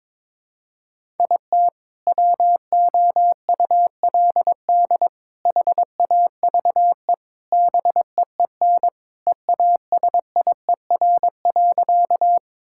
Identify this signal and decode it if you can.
Morse code, heard as a keyed tone, about 22 wpm, 700 Hz. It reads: IT WOULD HAVE BEEN EASIER.
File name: practice / 22wpm / Sentences with 4-Word Phrases - 22wpm